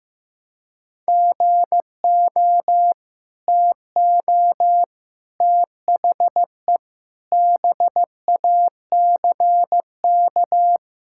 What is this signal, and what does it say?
Morse code, heard as a keyed tone, about 16 wpm, 700 Hz. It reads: GO TO THE BACK
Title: Sentences from Top 1000 Words - Review Part 5 - 15wpm